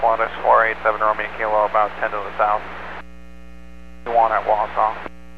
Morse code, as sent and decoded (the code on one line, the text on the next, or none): none